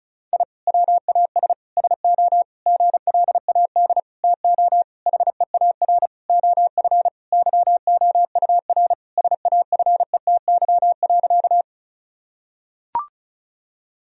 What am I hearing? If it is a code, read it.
Morse code, heard as a keyed tone, about 36 wpm, 700 Hz. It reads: I WAS SO GLAD TO HEAR OF YOUR SAFETY.
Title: Sentences from Top 1000 Words - Review Part 4 - 35wpm